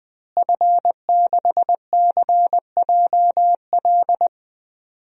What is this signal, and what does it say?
Morse code, heard as a keyed tone, about 20 wpm, 700 Hz. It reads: F6CJL
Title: DXCC-100 Call Signs - 5 Characters - Repeated 3x - 20wpm